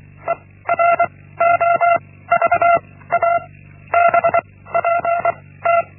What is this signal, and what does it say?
Morse code, heard as a keyed tone, about 24 wpm, 680 Hz. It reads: E R O V A B P T
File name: Linksmakalnis-1